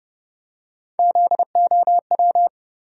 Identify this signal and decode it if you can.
Morse code, heard as a keyed tone, about 32 wpm, 700 Hz. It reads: ZOW
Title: USA Call Sign Postfixes - 3 Letters 30wpm